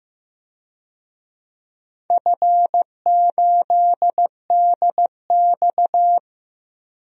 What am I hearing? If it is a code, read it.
Morse code, heard as a keyed tone, about 15 wpm, 700 Hz. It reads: F8DX
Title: DXCC-100 Call Signs - 4 Characters - Repeated 2x - 15wpm